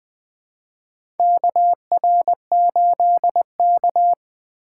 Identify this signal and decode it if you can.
Morse code, heard as a keyed tone, about 20 wpm, 700 Hz. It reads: KR8K
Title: USA Call Signs - 4 Characters - 2x Character Spacing - 20wpm